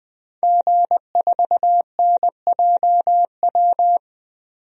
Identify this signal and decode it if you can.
Morse code, heard as a keyed tone, about 20 wpm, 700 Hz. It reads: G4NJW
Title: DXCC-100 Call Signs - 5 Characters - 20wpm